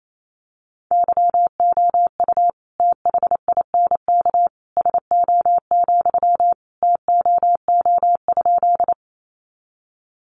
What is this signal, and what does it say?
Morse code, heard as a keyed tone, about 28 wpm, 700 Hz. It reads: YOU THINK SO, TOO?
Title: Sentences from Top 200 Words 28wpm